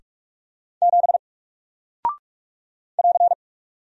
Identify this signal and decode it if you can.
Morse code, heard as a keyed tone, about 44 wpm, 700 Hz. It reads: 7 <AR>
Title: Single Character (Letter:Number:Punctuation:Prosign) Alt 45wpm